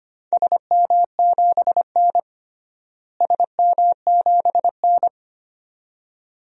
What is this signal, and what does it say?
Morse code, heard as a keyed tone, about 25 wpm, 700 Hz. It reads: SM7N SM7N
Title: DXCC-100 Call Signs - 4 Characters - Repeated 2x - 25wpm